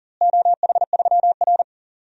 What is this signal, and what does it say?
Morse code, heard as a keyed tone, about 40 wpm, 700 Hz. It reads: OH3R
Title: DXCC-100 Call Signs - 4 Characters - Repeated 2x - 40wpm